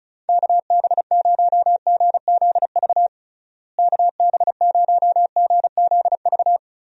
Morse code, heard as a keyed tone, about 34 wpm, 700 Hz. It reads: KB0GZV KB0GZV
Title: USA Call Signs - 6 Characters - Repeated 3x - 35wpm